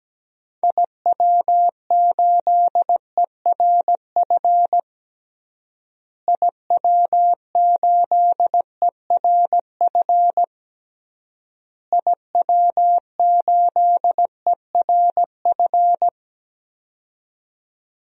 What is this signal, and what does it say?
Morse code, heard as a keyed tone, about 17 wpm, 700 Hz. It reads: IW8ERF IW8ERF IW8ERF
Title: DXCC-100 Call Signs - 6 Characters - Repeated 3x - 17wpm